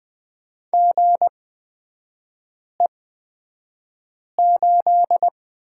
Morse code, heard as a keyed tone, about 20 wpm, spaced as for 5 wpm, 700 Hz. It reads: GE8
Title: 255 Lesson 41e - Period - Three Characters 20wpm